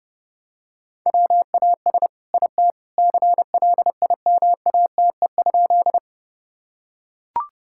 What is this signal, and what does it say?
Morse code, heard as a keyed tone, about 32 wpm, 700 Hz. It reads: WAS IT CLIMATE?
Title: Sentences from Top 1600 Words 30wpm